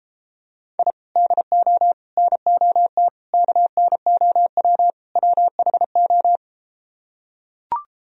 Morse code, heard as a keyed tone, about 33 wpm, 700 Hz. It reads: I DO NOT KNOW WHO